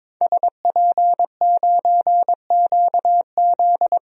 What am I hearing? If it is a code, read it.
Morse code, heard as a keyed tone, about 22 wpm, 700 Hz. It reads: SP9QZ
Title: DXCC-100 Call Signs - 5 Characters - Repeated 3x - 22wpm